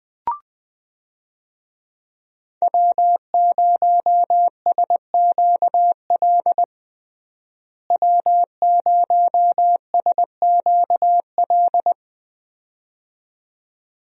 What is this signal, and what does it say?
Morse code, heard as a keyed tone, about 20 wpm, 700 Hz. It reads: W0SQL W0SQL
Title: USA Call Signs - 5 Characters - Repeated 2x - 20wpm